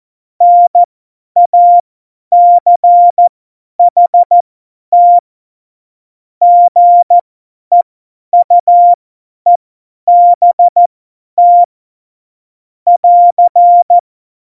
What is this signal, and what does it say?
Morse code, heard as a keyed tone, about 14 wpm, spaced as for 10 wpm, 700 Hz. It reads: NACHT GEUEBT <AR>
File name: Morsekurs / Witze / Witz 12. Dezember 2017, 50 F70